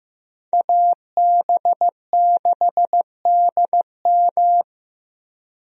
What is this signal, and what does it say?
Morse code, heard as a keyed tone, about 15 wpm, 700 Hz. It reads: AB6DM
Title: USA Call Signs - 5 Characters - Repeated 2x - 15wpm